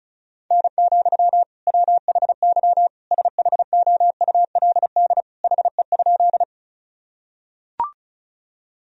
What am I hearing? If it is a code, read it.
Morse code, heard as a keyed tone, about 35 wpm, 700 Hz. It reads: N, WHY SHOULD HE?
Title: Sentences from Top 300 Words and Abbreviations 35wpm